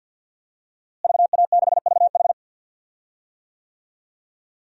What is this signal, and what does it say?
Morse code, heard as a keyed tone, about 46 wpm, 700 Hz. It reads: VA6VH
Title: DXCC-100 Call Signs - 5 Characters - Repeated 3x - 50wpm